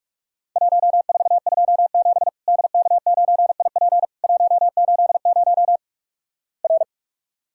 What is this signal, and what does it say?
Morse code, heard as a keyed tone, about 46 wpm, 700 Hz. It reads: 1427 DK9IP 180 R
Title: Worked All Europe DX Contest - QTC Exchange - 45wpm